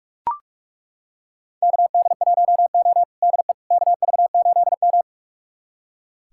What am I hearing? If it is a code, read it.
Morse code, heard as a keyed tone, about 42 wpm, 700 Hz. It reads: KD1O DE KV8M
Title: USA Callsign DE Callsign - 4 Characters - 45wpm